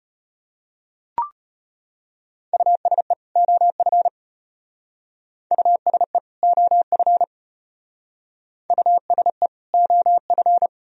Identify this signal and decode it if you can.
Morse code, heard as a keyed tone, about 33 wpm, 700 Hz. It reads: USE OF USE OF USE OF